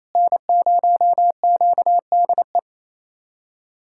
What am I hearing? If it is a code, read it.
Morse code, heard as a keyed tone, about 28 wpm, 700 Hz. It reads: N0QDE
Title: USA Callsign DE Callsign 28wpm